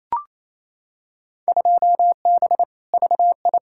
Morse code, heard as a keyed tone, about 28 wpm, 700 Hz. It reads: 2B VI
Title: Field Day Exchanges Encore - 28wpm